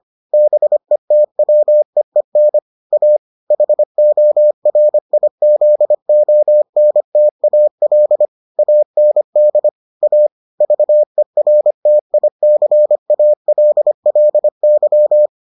Morse code, heard as a keyed tone, about 25 wpm, 600 Hz. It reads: BETWEEN A HORIZONTAL AND A VERTICALLY